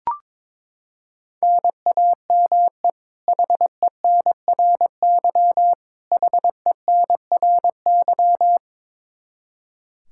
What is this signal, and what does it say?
Morse code, heard as a keyed tone, about 22 wpm, 700 Hz. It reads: NAME HENRY HENRY
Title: Practice QSO - Exchange 1 - Beginning - 22wpm